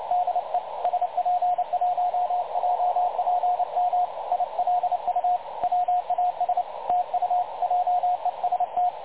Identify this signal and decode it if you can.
Morse code, heard as a keyed tone, about 29 wpm, 720 Hz, with radio noise. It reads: DE SP1JPM ILU WAS TU JEST
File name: dlazaawansowanych / nagraniazpasma / websdr recording 3565.0kHz